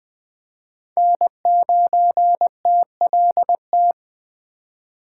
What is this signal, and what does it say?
Morse code, heard as a keyed tone, about 24 wpm, 700 Hz. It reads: N9TLT
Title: Speed Racing - USA 1x3 Call Signs 20wpm